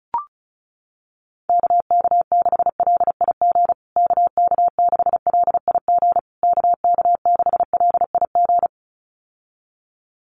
Morse code, heard as a keyed tone, about 34 wpm, 700 Hz. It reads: KK6LIG KK6LIG KK6LIG
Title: USA Call Signs Repeated 3x 35wpm